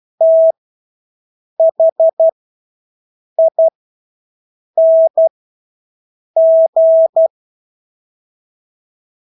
Morse code, heard as a keyed tone, about 12 wpm, spaced as for 6 wpm, 650 Hz. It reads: THING
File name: Engelske ord 12-6 wpm